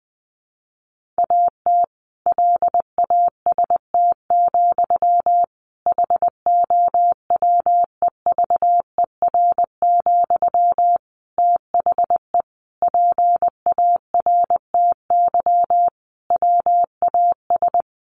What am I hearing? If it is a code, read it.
Morse code, heard as a keyed tone, about 20 wpm, 700 Hz. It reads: AT LAST, HOWEVER, THE PARTY WAS
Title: Sentences from Top 1400 Words 20wpm